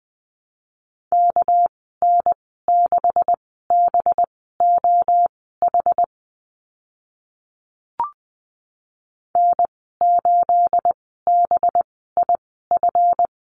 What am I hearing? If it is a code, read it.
Morse code, heard as a keyed tone, about 20 wpm, spaced as for 15 wpm, 700 Hz. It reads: KN6BOH N8BIF
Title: 213 Lesson 34g - Number 8 - Call Signs 20wpm